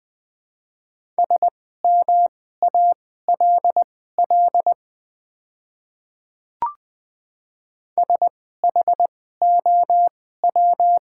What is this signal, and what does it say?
Morse code, heard as a keyed tone, about 20 wpm, spaced as for 14 wpm, 700 Hz. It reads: SMALL SHOW
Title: 2x Character Spacing - Top 200 Words - 20wpm